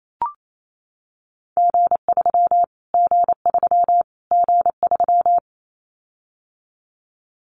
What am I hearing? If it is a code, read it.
Morse code, heard as a keyed tone, about 28 wpm, 700 Hz. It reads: G3 G3 G3
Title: Callsign Prefixes for Top 25 DXCC - Repeated 3x - 28wpm